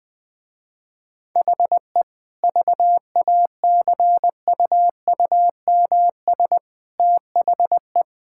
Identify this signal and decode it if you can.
Morse code, heard as a keyed tone, about 20 wpm, 700 Hz. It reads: HE VACUUMS THE
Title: Easy Sentences with Hints - 20wpm